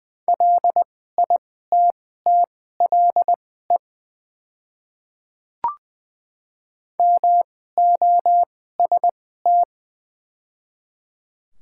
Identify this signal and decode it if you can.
Morse code, heard as a keyed tone, about 20 wpm, spaced as for 14 wpm, 700 Hz. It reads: LITTLE MOST
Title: 2x Character Spacing - Top 300 Words - 20wpm